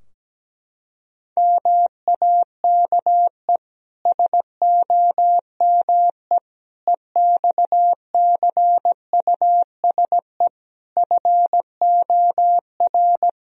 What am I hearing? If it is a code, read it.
Morse code, heard as a keyed tone, about 17 wpm, 700 Hz. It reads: MAKE SOME EXCUSE FOR
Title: Sentences from Top 1000 Words - Review Part 3 - 17wpm